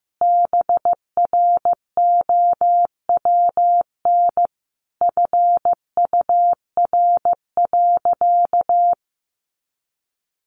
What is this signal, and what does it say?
Morse code, heard as a keyed tone, about 14 wpm, 700 Hz. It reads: BROWN FUR.
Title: Easy Phrases 15wpm